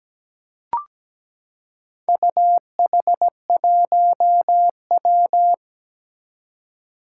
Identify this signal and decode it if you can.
Morse code, heard as a keyed tone, about 17 wpm, 700 Hz. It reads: UH1W
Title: Four Challenging Letter-Number Combinations 17wpm